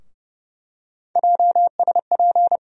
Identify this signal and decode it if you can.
Morse code, heard as a keyed tone, about 31 wpm, 700 Hz. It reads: JSP
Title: USA Call Sign Postfixes - 3 Letters 30wpm